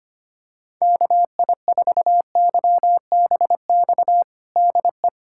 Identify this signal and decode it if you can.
Morse code, heard as a keyed tone, about 25 wpm, 700 Hz. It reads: KI4YBX DE
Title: USA Callsign DE Callsign - 6 Characters - 25wpm